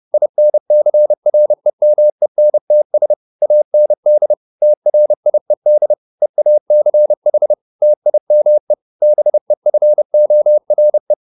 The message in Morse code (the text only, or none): INCREMENTS AND TRIED EACH TIME BEFORE